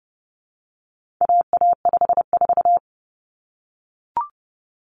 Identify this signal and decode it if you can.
Morse code, heard as a keyed tone, about 30 wpm, 700 Hz. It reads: AA54
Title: Common Cut Numbers - 4-Digit 30wpm